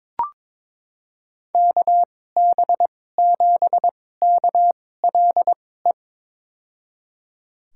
Morse code, heard as a keyed tone, about 23 wpm, spaced as for 16 wpm, 700 Hz. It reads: KB7KLE
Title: USA Call Signs - 6 Characters - 2x Character Spacing - 22wpm